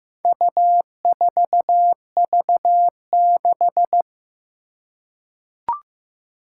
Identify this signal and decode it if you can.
Morse code, heard as a keyed tone, about 16 wpm, 700 Hz. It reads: U4V6
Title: Four Challenging Letter-Number Combinations 15wpm